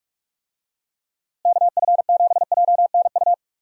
Morse code, heard as a keyed tone, about 46 wpm, 700 Hz. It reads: KF7JNU
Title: USA Call Signs - 6 Characters - Repeated 2x - 45wpm